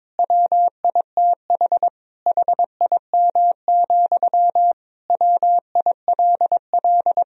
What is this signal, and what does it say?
Morse code, heard as a keyed tone, about 21 wpm, 700 Hz. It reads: WITH HIM, WILL
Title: Sentences from Top 1000 Words - Review Part 1 - 22wpm